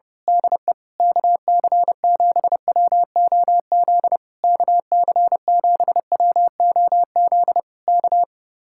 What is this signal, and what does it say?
Morse code, heard as a keyed tone, about 26 wpm, 700 Hz. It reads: DE KC7WOZ KC7WOZ K
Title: Speed-Racing Common QSO Phrases - 25wpm